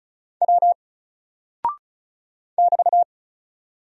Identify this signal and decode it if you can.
Morse code, heard as a keyed tone, about 34 wpm, 700 Hz. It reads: W <BT>
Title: Single Character (Letter:Number:Punctuation:Prosign) 35wpm